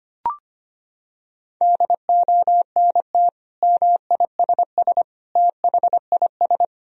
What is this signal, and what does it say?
Morse code, heard as a keyed tone, about 25 wpm, 700 Hz. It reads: DONT MISS THIS
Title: Sentences from Top 1000 Words - Review Part 3 - 25wpm